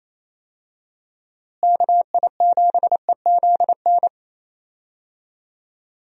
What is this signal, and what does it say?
Morse code, heard as a keyed tone, about 28 wpm, 700 Hz. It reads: KI7EZN
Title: USA Call Signs - 6 Characters - Repeated 2x - 28wpm